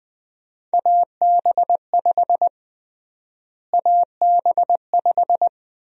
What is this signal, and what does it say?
Morse code, heard as a keyed tone, about 21 wpm, 700 Hz. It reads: AB5 AB5
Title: USA Call Sign Prefixes - Repeated 2x - 20wpm